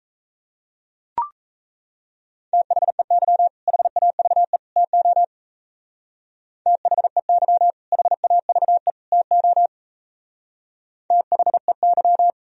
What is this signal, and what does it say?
Morse code, heard as a keyed tone, about 32 wpm, 700 Hz. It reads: THEY HAVE TO THEY HAVE TO THEY
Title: Speed-Racing - Sets of 3 Words 28wpm